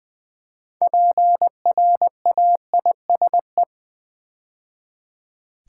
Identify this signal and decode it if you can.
Morse code, heard as a keyed tone, about 20 wpm, 700 Hz. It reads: PRAISE
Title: Build A Word - 2 Letters At A Time - 6-Letter Words - 20wpm